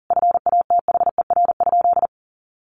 Morse code, heard as a keyed tone, about 40 wpm, 700 Hz. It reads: FATHER?
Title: Sentences from Top 1200 Words 40wpm